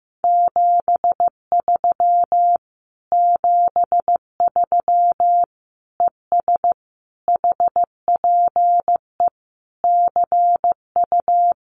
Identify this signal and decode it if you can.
Morse code, heard as a keyed tone, about 15 wpm, 700 Hz. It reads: 73 73 ES HPE CU
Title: Common QSO Phrases Encore 15wpm